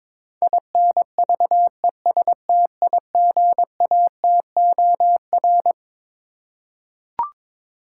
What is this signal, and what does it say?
Morse code, heard as a keyed tone, about 22 wpm, 700 Hz. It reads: INVESTIGATOR
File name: Twelve Letter Words - 22wpm